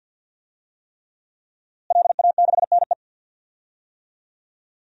Morse code, heard as a keyed tone, about 47 wpm, 700 Hz. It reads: LA6NE
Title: DXCC-100 Call Signs - 5 Characters - Repeated 3x - 50wpm